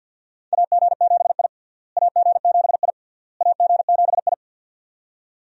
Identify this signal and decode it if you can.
Morse code, heard as a keyed tone, about 46 wpm, 700 Hz. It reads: AG7I AG7I AG7I
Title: USA Call Signs - 4 Characters - Repeated 3x - 50wpm